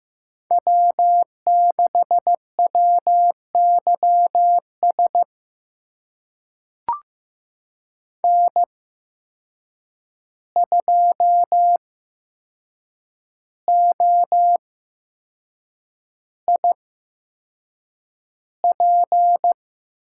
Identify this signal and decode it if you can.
Morse code, heard as a keyed tone, about 15 wpm, spaced as for 6 wpm, 700 Hz. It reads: W6WYS N2OIP
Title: USA Call Signs - 8x Character Spacing - 15wpm